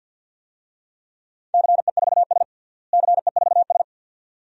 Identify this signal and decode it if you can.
Morse code, heard as a keyed tone, about 47 wpm, 700 Hz. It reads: CE4S CE4S
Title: DXCC-100 Call Signs - 4 Characters - Repeated 2x - 50wpm